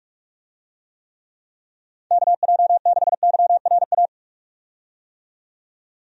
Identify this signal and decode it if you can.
Morse code, heard as a keyed tone, about 47 wpm, 700 Hz. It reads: KJ6YRA
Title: USA Call Signs - 6 Characters - 45wpm